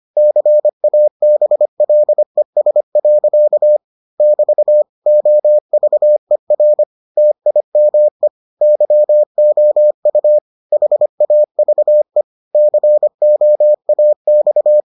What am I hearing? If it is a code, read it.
Morse code, heard as a keyed tone, about 25 wpm, 600 Hz. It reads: CABLES. <BT> OVER TIME YOU HAVE COAX